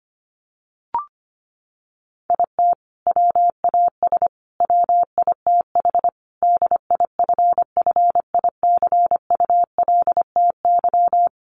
Morse code, heard as a keyed tone, about 25 wpm, 700 Hz. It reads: IT WAS WITH DIFFICULTY
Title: Sentences from Top 1500 Words 25wpm